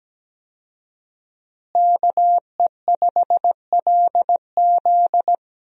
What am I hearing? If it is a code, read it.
Morse code, heard as a keyed tone, about 17 wpm, 700 Hz. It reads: KE5LZ
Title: USA Call Signs - 5 Characters - Repeated 3x - 17wpm